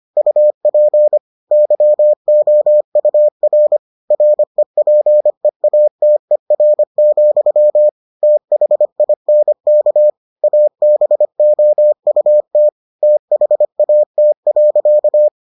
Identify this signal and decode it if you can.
Morse code, heard as a keyed tone, about 25 wpm, 600 Hz. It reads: UP YOUR REPEATER, THINK ABOUT THAT.